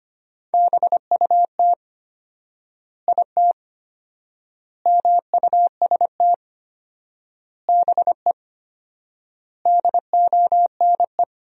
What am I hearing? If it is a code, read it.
Morse code, heard as a keyed tone, about 25 wpm, 700 Hz. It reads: BUT IT MUST BE DONE
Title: 4x Word-Spacing - Sentences from Top 500 Words - 25wpm